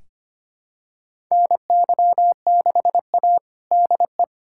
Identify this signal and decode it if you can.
Morse code, heard as a keyed tone, about 25 wpm, 700 Hz. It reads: NY6A DE
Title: USA Callsign DE Callsign - 4 Characters - 25wpm